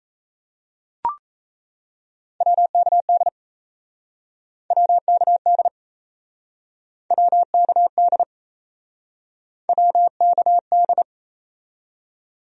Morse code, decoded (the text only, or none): WKD WKD WKD WKD